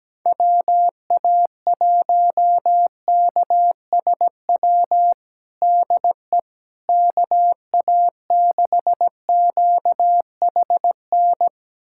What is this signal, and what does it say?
Morse code, heard as a keyed tone, about 17 wpm, 700 Hz. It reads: WA1KSW DE KA6QHN
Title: USA Callsign DE Callsign - 6 Characters - 17wpm